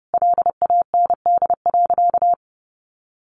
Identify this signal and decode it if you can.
Morse code, heard as a keyed tone, about 30 wpm, 700 Hz. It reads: LAND.
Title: Sentences from Top 1500 Words 30wpm